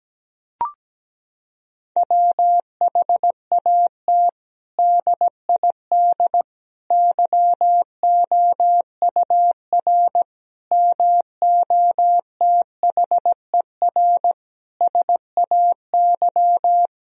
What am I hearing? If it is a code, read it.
Morse code, heard as a keyed tone, about 17 wpm, 700 Hz. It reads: WHAT DID YOUR MOTHER SAY
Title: Sentences from Top 1000 Words - Review Part 3 - 17wpm